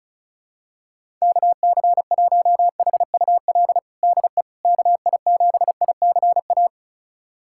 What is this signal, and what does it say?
Morse code, heard as a keyed tone, about 36 wpm, 700 Hz. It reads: KC1HUL DE KI7ICA
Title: USA Callsign DE Callsign - 6 Characters - 35wpm